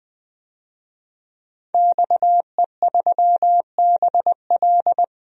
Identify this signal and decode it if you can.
Morse code, heard as a keyed tone, about 20 wpm, 700 Hz. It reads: XE3BL
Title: DXCC-100 Call Signs - 5 Characters - 20wpm